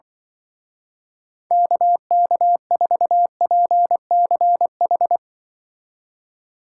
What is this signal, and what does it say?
Morse code, heard as a keyed tone, about 24 wpm, 700 Hz. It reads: KK4PCH
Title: Speed Racing - USA 2x3 Call Signs 20wpm